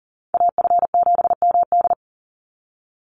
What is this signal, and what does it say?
Morse code, heard as a keyed tone, about 40 wpm, 700 Hz. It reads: AF7MD
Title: NA SOTA Chaser Callsigns 40wpm